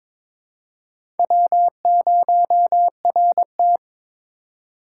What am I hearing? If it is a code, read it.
Morse code, heard as a keyed tone, about 22 wpm, 700 Hz. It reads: W0RT
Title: USA Call Signs - 6x Character Spacing - 22wpm